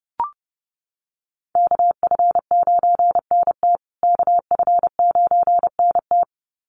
Morse code, heard as a keyed tone, about 30 wpm, 700 Hz. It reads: KF9NT KF9NT
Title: USA Call Signs Repeated 2x 30wpm